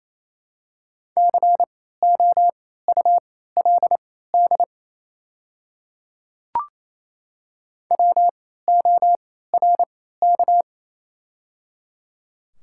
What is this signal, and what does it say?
Morse code, heard as a keyed tone, about 28 wpm, spaced as for 16 wpm, 700 Hz. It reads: COULD WORK
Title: 3x Character Spacing - Top 300 Words - 28wpm